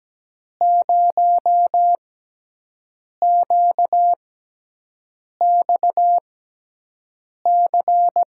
Morse code, heard as a keyed tone, about 17 wpm, spaced as for 6 wpm, 700 Hz. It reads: 0QXC
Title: USA Call Signs - 5 Characters - 6x Character Spacing - 17wpm